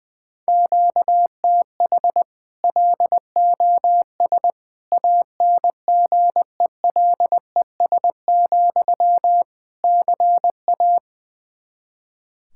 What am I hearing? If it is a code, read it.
Morse code, heard as a keyed tone, about 21 wpm, 700 Hz. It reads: QTH LOS ANGELES, CA
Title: Practice QSO - Exchange 1 - Beginning - 20wpm